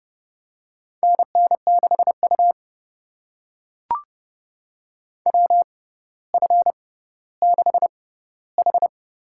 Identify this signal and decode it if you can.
Morse code, heard as a keyed tone, about 30 wpm, spaced as for 15 wpm, 700 Hz. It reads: NN6U WF6H
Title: USA Call Signs - 4 Characters - 6x Character Spacing - 30wpm